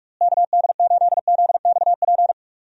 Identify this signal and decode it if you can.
Morse code, heard as a keyed tone, about 46 wpm, 700 Hz. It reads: KD8ZXP
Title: USA Call Signs - 6 Characters - Repeated 3x - 45wpm